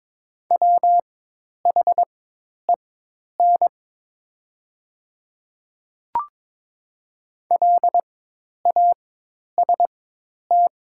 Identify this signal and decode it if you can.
Morse code, heard as a keyed tone, about 22 wpm, spaced as for 10 wpm, 700 Hz. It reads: WHEN LAST
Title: 4x Character Spacing - Top 500 Words - 22wpm